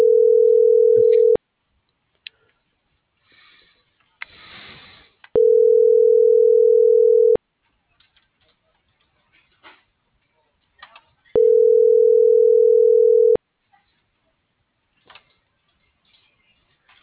Ringback tone (North America), 440 + 480 Hz.